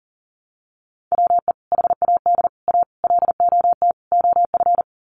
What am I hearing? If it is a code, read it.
Morse code, heard as a keyed tone, about 40 wpm, 700 Hz. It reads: WE HAD A LOT OF
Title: Sets of 5 Words Encore 40wpm